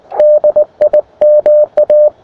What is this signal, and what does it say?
Morse code, heard as a keyed tone, about 20 wpm, 600 Hz. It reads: DIMA
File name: attachments / europe 20m